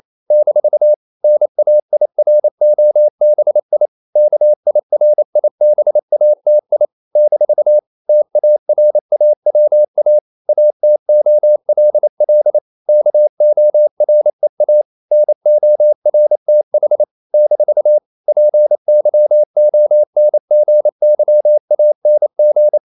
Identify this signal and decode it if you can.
Morse code, heard as a keyed tone, about 28 wpm, 600 Hz. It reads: - NAIROBI KIRIBATI - TARAWA ATOLL KOREA NORTH - PYONGYANG
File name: CW / Practice / NationalCapitals28